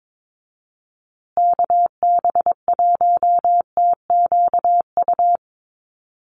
Morse code, heard as a keyed tone, about 22 wpm, 700 Hz. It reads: KB1TQU
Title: USA Call Signs Repeated 2x 22wpm